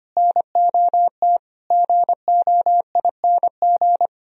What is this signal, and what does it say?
Morse code, heard as a keyed tone, about 31 wpm, 700 Hz. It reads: NOT GOING
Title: Speed-Racing - Sets of 4 Words 25wpm